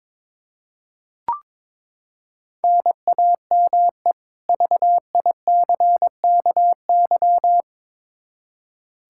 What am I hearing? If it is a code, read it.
Morse code, heard as a keyed tone, about 22 wpm, 700 Hz. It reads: NAME VICKY
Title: Practice QSO - Exchange 1 - Beginning - 22wpm